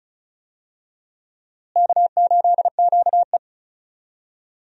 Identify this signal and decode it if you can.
Morse code, heard as a keyed tone, about 34 wpm, 700 Hz. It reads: K8QE
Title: USA Call Signs - 4 Characters - Repeated 2x - 35wpm